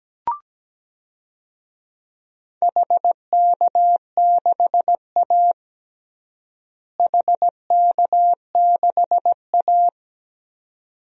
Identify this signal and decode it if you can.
Morse code, heard as a keyed tone, about 17 wpm, 700 Hz. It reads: HK6A HK6A
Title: DXCC-100 Call Signs - 4 Characters - Repeated 2x - 17wpm